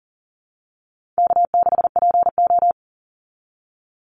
Morse code, heard as a keyed tone, about 40 wpm, 700 Hz. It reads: K6PO
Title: Easy Call Signs Repeated 2x - 40wpm